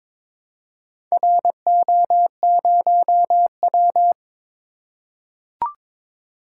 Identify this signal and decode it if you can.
Morse code, heard as a keyed tone, about 22 wpm, 700 Hz. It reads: RO0W
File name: practice / 22wpm / Four Letter-Number Combinations - 22wpm